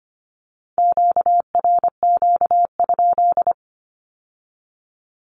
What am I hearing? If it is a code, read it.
Morse code, heard as a keyed tone, about 25 wpm, 700 Hz. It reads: QRQ?
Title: Q-Codes with Spoken Meaning - 25wpm